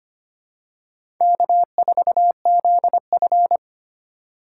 Morse code, heard as a keyed tone, about 25 wpm, 700 Hz. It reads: K4ZF
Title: USA Call Signs - 4 Characters - 4x Character Spacing - 25wpm